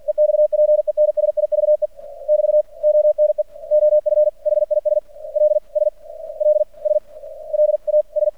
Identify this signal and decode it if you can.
Morse code, heard as a keyed tone, about 47 wpm, spaced as for 35 wpm, 600 Hz. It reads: EXPENSIVE <BT> ONE OF HIS LSLSLAS
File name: archived / CRSnet